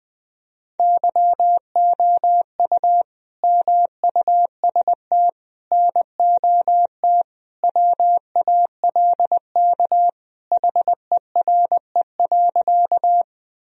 Morse code, heard as a keyed tone, about 20 wpm, 700 Hz. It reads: YOU MUST NOT WALK HERE.
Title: Sentences from Top 1000 Words - Review Part 1 - 20wpm